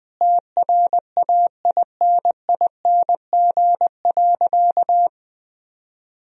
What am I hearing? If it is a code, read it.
Morse code, heard as a keyed tone, about 20 wpm, 700 Hz. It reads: TRAINING.
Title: Sentences from Top 1000 Words - Review Part 4 - 20wpm